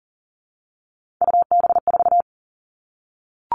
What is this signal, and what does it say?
Morse code, heard as a keyed tone, about 40 wpm, 700 Hz. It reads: UB4
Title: All Cut Numbers - 3-Digit 40wpm